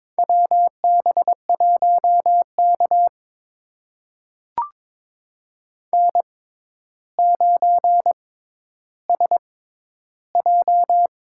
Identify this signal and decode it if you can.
Morse code, heard as a keyed tone, about 22 wpm, spaced as for 11 wpm, 700 Hz. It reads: WB1K N9SJ
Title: USA Call Signs - 4 Characters - 6x Character Spacing - 22wpm